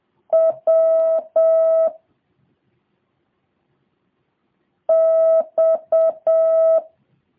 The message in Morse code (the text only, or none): W X